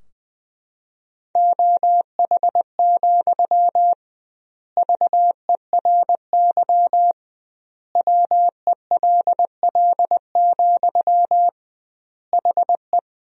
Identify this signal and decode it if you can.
Morse code, heard as a keyed tone, about 20 wpm, 700 Hz. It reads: OH, VERY WELL, HE
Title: 2x Word-Spacing - Sentences from Top 500 Words - 20wpm